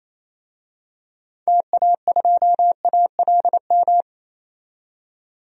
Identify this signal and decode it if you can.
Morse code, heard as a keyed tone, about 28 wpm, 700 Hz. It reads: TA2ALM
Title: DXCC-100 Call Signs - 6 Characters - 28wpm